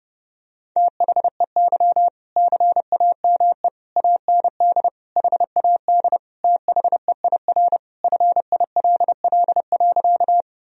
Morse code, heard as a keyed tone, about 30 wpm, 700 Hz. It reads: THEY CAME AND HAD THEIR FILL.
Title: Sentences from Top 1000 Words - Review Part 3 - 30wpm